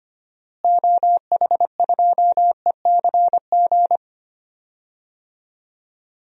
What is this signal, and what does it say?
Morse code, heard as a keyed tone, about 25 wpm, 700 Hz. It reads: OH2ECG
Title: DXCC-100 Call Signs - 6 Characters - Repeated 3x - 25wpm